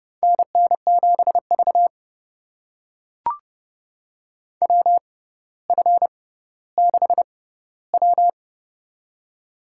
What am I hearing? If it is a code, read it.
Morse code, heard as a keyed tone, about 30 wpm, spaced as for 15 wpm, 700 Hz. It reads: NN7V WF6W
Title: USA Call Signs - 4 Characters - 6x Character Spacing - 30wpm